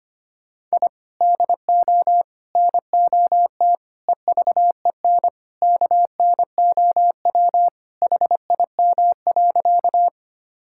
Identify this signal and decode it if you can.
Morse code, heard as a keyed tone, about 25 wpm, 700 Hz. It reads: I DO NOT EVEN KNOW HIM.
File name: practice / 25wpm / Sentences with 4-Word Phrases - 25wpm